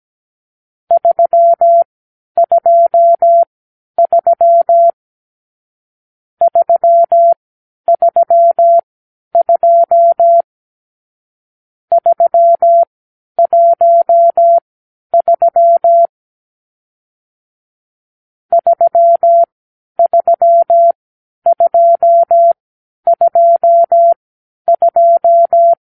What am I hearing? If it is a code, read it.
Morse code, heard as a keyed tone, about 17 wpm, spaced as for 10 wpm, 680 Hz. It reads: 323 332 313 33222
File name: k7qo / Lesson 44 (1-3)